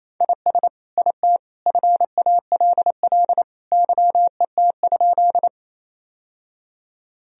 Morse code, heard as a keyed tone, about 28 wpm, 700 Hz. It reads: IS IT FALL YET?
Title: Sentences from Top 1000 Words - Review Part 2 - 28wpm